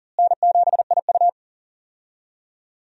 Morse code, heard as a keyed tone, about 40 wpm, 700 Hz. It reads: N7IU
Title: USA Call Signs - 4 Characters - Repeated 3x - 40wpm